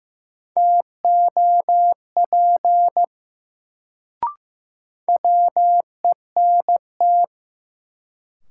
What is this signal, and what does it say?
Morse code, heard as a keyed tone, about 15 wpm, 700 Hz. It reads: TOP WENT
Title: Sets of 2 Words - New Word Drill - 15wpm